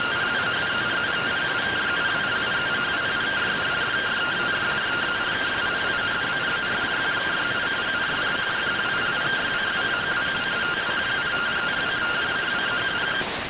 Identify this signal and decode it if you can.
Two-tone teleprinter signal, about 170 Hz shift.